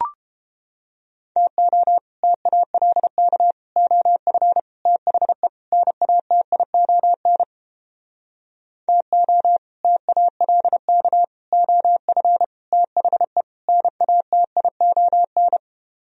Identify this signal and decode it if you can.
Morse code, heard as a keyed tone, about 29 wpm, 700 Hz. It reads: TO TALK OF THE NATION TO TALK OF THE NATION